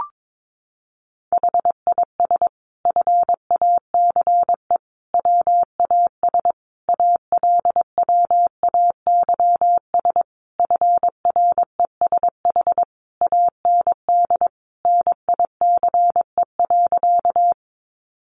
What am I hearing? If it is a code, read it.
Morse code, heard as a keyed tone, about 22 wpm, 700 Hz. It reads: HIS FACE WAS ALWAYS FRESH AND NICE.
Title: Sentences from Top 1200 Words 22wpm